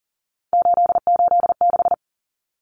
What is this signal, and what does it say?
Morse code, heard as a keyed tone, about 40 wpm, 700 Hz. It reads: 886
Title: Numbers - 3-Digit 40wpm